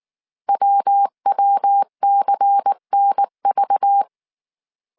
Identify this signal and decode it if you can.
Morse code, heard as a keyed tone, about 19 wpm, 800 Hz. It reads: WWCNV
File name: CW / DET / 1 6 1500C S C 18wpm 2356